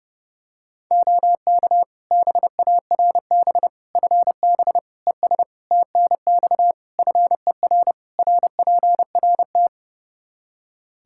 Morse code, heard as a keyed tone, about 32 wpm, 700 Hz. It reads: OK BARB FB ES TNX FER RPRT
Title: Practice QSO - Exchange 2 - Info - 30wpm